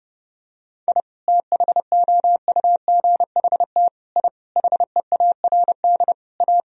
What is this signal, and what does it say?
Morse code, heard as a keyed tone, about 30 wpm, 700 Hz. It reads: I THOUGHT I HEARD A
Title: Sentences from Top 1000 Words - Review Part 1 - 30wpm